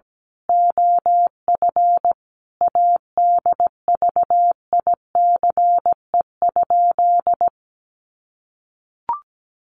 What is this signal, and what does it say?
Morse code, heard as a keyed tone, about 17 wpm, 700 Hz. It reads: OF ADVICE?
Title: Sentences from Top 1100 Words 17wpm